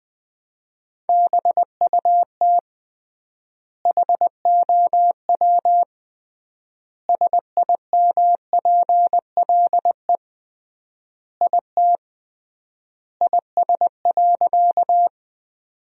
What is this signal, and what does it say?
Morse code, heard as a keyed tone, about 21 wpm, 700 Hz. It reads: BUT HOW SIMPLE IT IS.
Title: 3x Word-Spacing - Sentences from Top 400 Words - 20wpm